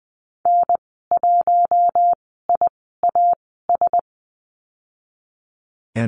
Morse code, heard as a keyed tone, about 20 wpm, spaced as for 15 wpm, 700 Hz. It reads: N1IAS
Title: 062 Lesson 11g - Letter h - Call Signs 20wpm